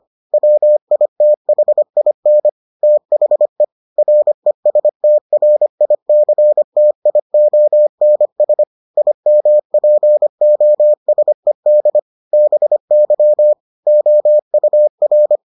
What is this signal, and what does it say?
Morse code, heard as a keyed tone, about 25 wpm, 600 Hz. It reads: WITHIN THE RESTRICTIONS IMPOSED BY OUR